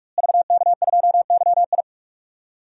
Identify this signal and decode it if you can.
Morse code, heard as a keyed tone, about 46 wpm, 700 Hz. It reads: VK2YI
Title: DXCC-100 Call Signs - 5 Characters - Repeated 2x - 45wpm